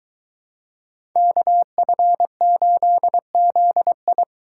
Morse code, heard as a keyed tone, about 22 wpm, 700 Hz. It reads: KF8ZI